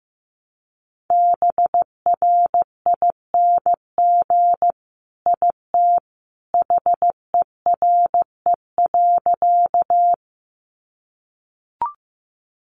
Morse code, heard as a keyed tone, about 15 wpm, 700 Hz. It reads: BRING IT HERE.
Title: Sentences from Top 500 Words 15wpm